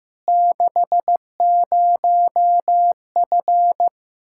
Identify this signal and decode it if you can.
Morse code, heard as a keyed tone, about 16 wpm, 700 Hz. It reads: 60F
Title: Practice QSO - Exchange 2 - Info - 15wpm